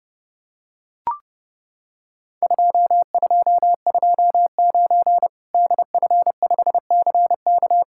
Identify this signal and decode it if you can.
Morse code, heard as a keyed tone, about 30 wpm, 700 Hz. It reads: 2229 DF5CK
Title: Worked All Europe DX Contest - QTC Exchange - 30wpm